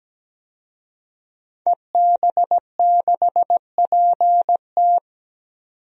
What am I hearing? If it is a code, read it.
Morse code, heard as a keyed tone, about 17 wpm, 700 Hz. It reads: EB6PT
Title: DXCC-100 Call Signs - 5 Characters - 17wpm